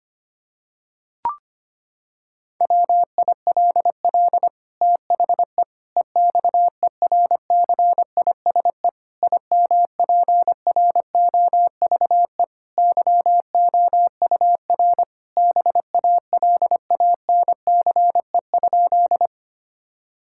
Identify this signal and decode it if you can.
Morse code, heard as a keyed tone, about 25 wpm, 700 Hz. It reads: WILL THE EXERCISE IMPROVE YOUR BALANCE?
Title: Sentences from Top 1000 Words - Review Part 6 - 25wpm